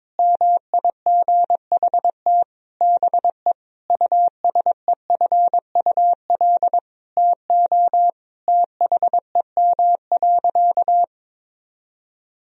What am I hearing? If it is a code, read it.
Morse code, heard as a keyed tone, about 21 wpm, 700 Hz. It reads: MIGHT BE USEFUL TO THEM.
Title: Sentences from Top 1000 Words - Review Part 4 - 22wpm